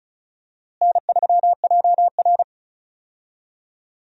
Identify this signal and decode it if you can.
Morse code, heard as a keyed tone, about 36 wpm, 700 Hz. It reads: N3JR
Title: USA Call Signs - 4 Characters - Repeated 3x - 35wpm